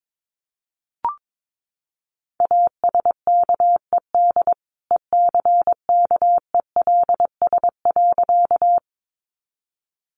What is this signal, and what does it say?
Morse code, heard as a keyed tone, about 22 wpm, 700 Hz. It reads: ASKED ECKELS.
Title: A Sound Of Thunder 22wpm